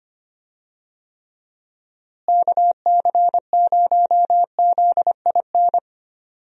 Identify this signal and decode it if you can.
Morse code, heard as a keyed tone, about 25 wpm, 700 Hz. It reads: KC0ZIN
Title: USA Call Signs - 6 Characters - Repeated 3x - 25wpm